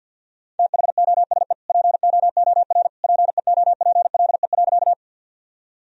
Morse code, heard as a keyed tone, about 47 wpm, 700 Hz. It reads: THOSE POOR PEOPLE.
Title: Sentences from Top 1000 Words - Review Part 1 - 50wpm